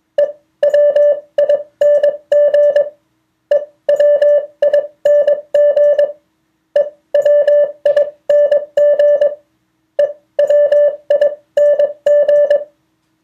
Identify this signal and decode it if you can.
Morse code, heard as a keyed tone, about 22 wpm, spaced as for 18 wpm, 580 Hz. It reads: EWING EWING EWING EWING